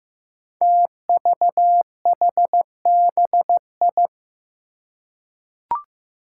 Morse code, heard as a keyed tone, about 15 wpm, 700 Hz. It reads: TVHBI
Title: Five Challenging Letter-Number Combinations 15wpm